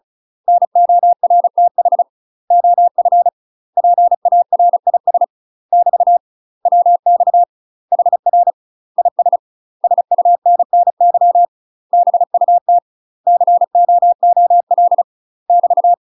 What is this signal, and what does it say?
Morse code, heard as a keyed tone, about 34 wpm, 700 Hz. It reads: NORTH OF PARIS <BT> WX HR IS SUNNY BUT COOL <BT>
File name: UK MASTER QSOs TRAINING / 35WPM / UK MASTER QSOs 0039